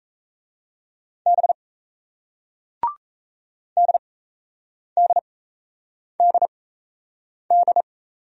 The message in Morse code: B DDDD

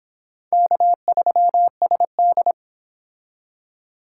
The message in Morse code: K3SD

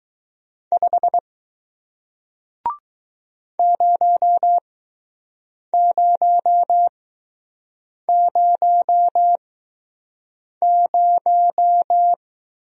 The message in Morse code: 5 0000

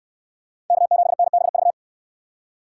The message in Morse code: K7NBV

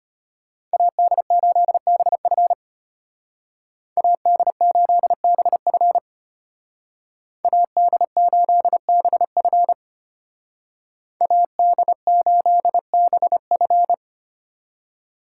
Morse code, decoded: AD8BF AD8BF AD8BF AD8BF

